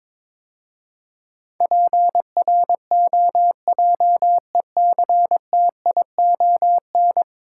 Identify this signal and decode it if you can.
Morse code, heard as a keyed tone, about 22 wpm, 700 Hz. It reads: PROJECTION
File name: practice / 22wpm / 10 Letter Words - First 3 Letters Given - 22wpm